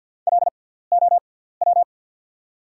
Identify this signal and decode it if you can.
Morse code, heard as a keyed tone, about 48 wpm, spaced as for 17 wpm, 690 Hz. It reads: LOW
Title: 6x Character Spacing - Top 200 Words - 50wpm